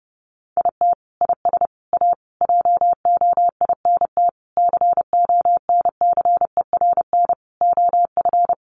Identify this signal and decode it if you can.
Morse code, heard as a keyed tone, about 30 wpm, 700 Hz. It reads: IT IS A JOINT CONCERN OF
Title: Sentences from Top 1300 Words 30wpm